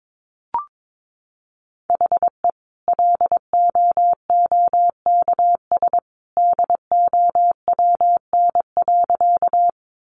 Morse code, heard as a keyed tone, about 22 wpm, 700 Hz. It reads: HE LOOKS DOWN.
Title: Additional Easy Phrases 22wpm